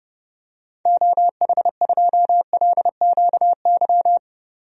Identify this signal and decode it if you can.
Morse code, heard as a keyed tone, about 30 wpm, 700 Hz. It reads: OH2LQY
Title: DXCC-100 Call Signs - 6 Characters - 30wpm